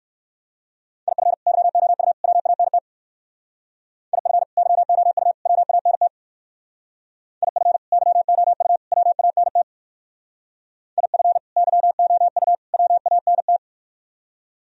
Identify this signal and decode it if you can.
Morse code, heard as a keyed tone, about 55 wpm, 700 Hz. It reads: IF YOU WANT IF YOU WANT IF YOU WANT IF YOU WANT
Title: Speed-Racing - Sets of 3 Words 45wpm